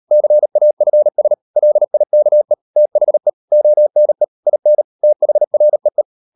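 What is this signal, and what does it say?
Morse code, heard as a keyed tone, about 38 wpm, 600 Hz. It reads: CAFS LIKE THE ONE IN THREE
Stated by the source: Morse code practice at 38 WPM, sending Ken Follett's novel Never in English, from CW med Gnister.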